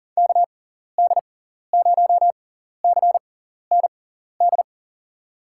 Morse code, heard as a keyed tone, about 40 wpm, spaced as for 14 wpm, 700 Hz. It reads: KD0CND
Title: USA Call Signs - 6x Character Spacing - 40wpm